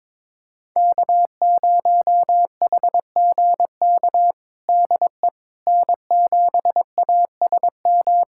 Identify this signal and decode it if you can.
Morse code, heard as a keyed tone, about 22 wpm, 700 Hz. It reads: K0HGK DE N7ASM
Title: USA Callsign DE Callsign - 5 Characters - 22wpm